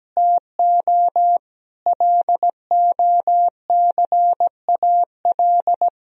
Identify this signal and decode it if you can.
Morse code, heard as a keyed tone, about 17 wpm, 700 Hz. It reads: TO LOCAL
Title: Sentences from Top 1000 Words - Review Part 4 - 17wpm